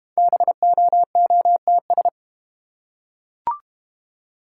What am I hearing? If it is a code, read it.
Morse code, heard as a keyed tone, about 31 wpm, 690 Hz. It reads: BOOTS